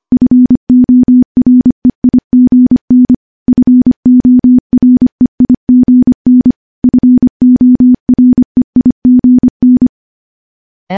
Morse code, heard as a keyed tone, about 25 wpm, 250 Hz. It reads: FOREIGN FOREIGN FOREIGN
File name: morse / morse-training-25WPM